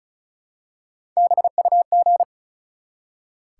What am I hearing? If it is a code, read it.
Morse code, heard as a keyed tone, about 36 wpm, 700 Hz. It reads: BUG
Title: Three Letter Words - 35wpm